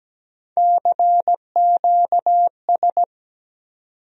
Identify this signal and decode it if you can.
Morse code, heard as a keyed tone, about 17 wpm, 700 Hz. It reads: CQS
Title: USA Call Sign Postfixes - 3 Letters 17wpm